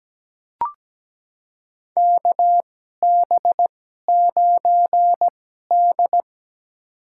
Morse code, heard as a keyed tone, about 17 wpm, spaced as for 12 wpm, 700 Hz. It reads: KB9D
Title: USA Call Signs - 4 Characters - 2x Character Spacing - 17wpm